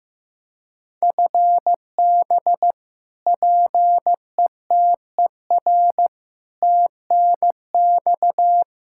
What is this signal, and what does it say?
Morse code, heard as a keyed tone, about 15 wpm, 700 Hz. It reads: FB PETER TNX
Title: Practice QSO - Exchange 3 - More Info - 15wpm